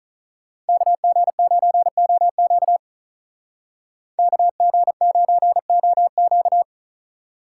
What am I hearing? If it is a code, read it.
Morse code, heard as a keyed tone, about 47 wpm, 700 Hz. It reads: KG9OQ KG9OQ